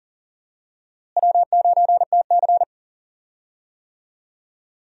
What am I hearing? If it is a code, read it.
Morse code, heard as a keyed tone, about 40 wpm, 700 Hz. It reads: W9TC
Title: USA Call Signs - 4 Characters - Repeated 2x - 40wpm